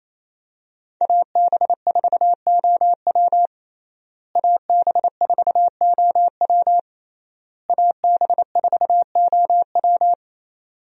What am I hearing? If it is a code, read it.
Morse code, heard as a keyed tone, about 27 wpm, 700 Hz. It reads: AB4OW AB4OW AB4OW
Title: USA Call Signs - 5 Characters - Repeated 3x - 28wpm